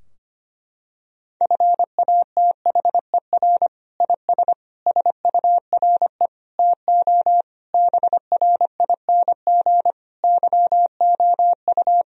Morse code, heard as a keyed tone, about 25 wpm, 700 Hz. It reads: FATHER IS SURE TO BRING YOU
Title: Sentences from Top 1000 Words - Review Part 3 - 25wpm